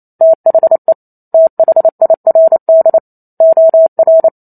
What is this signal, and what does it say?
Morse code, heard as a keyed tone, about 28 wpm, 650 Hz. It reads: THE THIRD OR